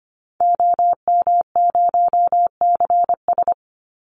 Morse code, heard as a keyed tone, about 25 wpm, 700 Hz. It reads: OM0CS
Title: Call Signs Repeated 3x - 25wpm